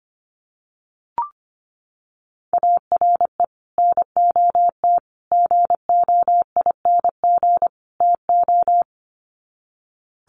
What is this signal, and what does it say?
Morse code, heard as a keyed tone, about 25 wpm, 700 Hz. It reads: ARE NOT GOING TO
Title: Sets of 4 Words 25wpm